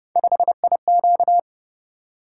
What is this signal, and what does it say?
Morse code, heard as a keyed tone, about 29 wpm, 700 Hz. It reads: 5IQ